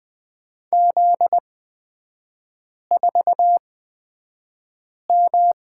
Morse code, heard as a keyed tone, about 20 wpm, spaced as for 5 wpm, 690 Hz. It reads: Z4M
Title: 255 Lesson 41e - Period - Three Characters 20wpm